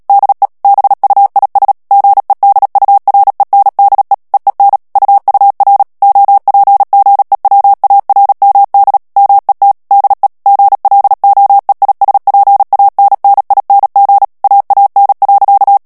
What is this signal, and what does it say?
Morse code, heard as a keyed tone, about 37 wpm, 800 Hz. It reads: DE BUIS GEDURENDE EEN UUR OPGEWARMD MET DE GLOEISPANNING AAN.
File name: morse mp3 / HOLLANDAIS texte clair 2 vitesse 35 mots minute